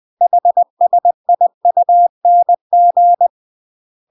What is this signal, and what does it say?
Morse code, heard as a keyed tone, about 20 wpm, 700 Hz. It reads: HSIUNG
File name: UK MASTER QSOs TRAINING / 20WPM / UK MASTER QSOs 0049